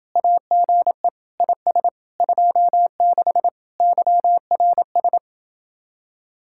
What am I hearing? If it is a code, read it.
Morse code, heard as a keyed tone, about 29 wpm, 700 Hz. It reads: AGE IS 26 YRS